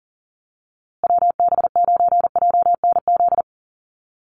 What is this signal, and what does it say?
Morse code, heard as a keyed tone, about 40 wpm, 700 Hz. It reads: WB9JNZ
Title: USA Call Signs Repeated 2x 40wpm